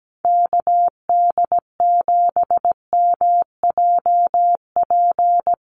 Morse code, heard as a keyed tone, about 17 wpm, 700 Hz. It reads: KD7MJP
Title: USA Call Signs Repeated 3x 17wpm